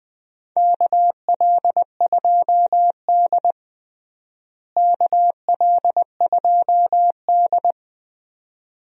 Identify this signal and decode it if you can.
Morse code, heard as a keyed tone, about 21 wpm, 700 Hz. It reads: KL2D KL2D
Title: USA Call Signs - 4 Characters - Repeated 2x - 20wpm